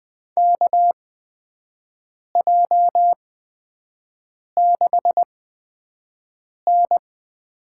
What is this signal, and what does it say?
Morse code, heard as a keyed tone, about 20 wpm, spaced as for 5 wpm, 700 Hz. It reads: KJ6N
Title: USA Call Signs - 6 Characters - 8x Character Spacing - 20wpm